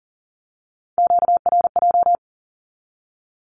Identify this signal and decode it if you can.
Morse code, heard as a keyed tone, about 42 wpm, 700 Hz. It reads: QRJ
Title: Q-Codes with Spoken Meaning - 40wpm